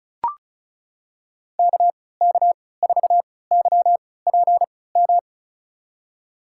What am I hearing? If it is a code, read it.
Morse code, heard as a keyed tone, about 34 wpm, spaced as for 20 wpm, 700 Hz. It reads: KK4YPM
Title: USA Call Signs - 6 Characters - 3x Character Spacing - 35wpm